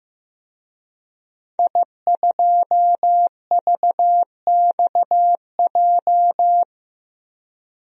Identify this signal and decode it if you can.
Morse code, heard as a keyed tone, about 15 wpm, 700 Hz. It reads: I2VXJ
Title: DXCC-100 Call Signs - 5 Characters - 15wpm